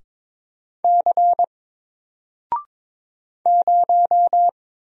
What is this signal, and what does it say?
Morse code, heard as a keyed tone, about 22 wpm, 700 Hz. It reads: C 0
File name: Single-Letter-Number-22wpm